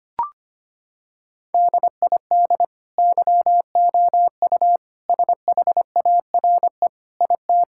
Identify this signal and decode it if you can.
Morse code, heard as a keyed tone, about 25 wpm, 700 Hz. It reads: DID YOU SHARE IT
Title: Sentences from Top 1000 Words - Review Part 6 - 25wpm